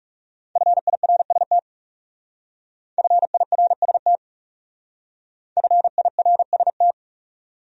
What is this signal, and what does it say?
Morse code, heard as a keyed tone, about 39 wpm, 700 Hz. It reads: FIRST FIRST FIRST